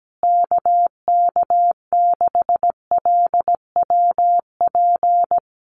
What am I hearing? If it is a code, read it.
Morse code, heard as a keyed tone, about 17 wpm, 700 Hz. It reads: KK6LWP
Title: USA Call Signs Repeated 2x 17wpm